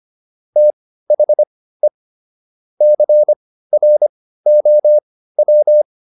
Morse code, heard as a keyed tone, about 25 wpm, spaced as for 15 wpm, 600 Hz. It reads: THE CROW